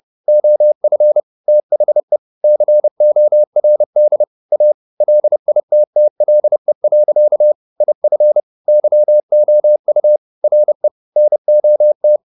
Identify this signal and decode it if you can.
Morse code, heard as a keyed tone, about 30 wpm, 600 Hz. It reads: OF THE CORD A LITTLE. IF YOU RE NOT
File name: the old man and the sea / the old man and the sea 1 30wpm